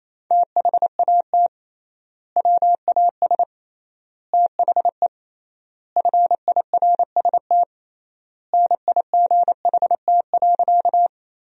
Morse code, heard as a keyed tone, about 28 wpm, 700 Hz. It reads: THAT WAS THE FIRST NIGHT.
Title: 3x Word-Spacing - Sentences from Top 300 Words - 28wpm